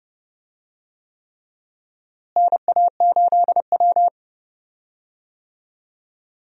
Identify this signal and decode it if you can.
Morse code, heard as a keyed tone, about 30 wpm, 700 Hz. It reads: NA8W
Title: USA Call Signs - 4 Characters - Repeated 3x - 30wpm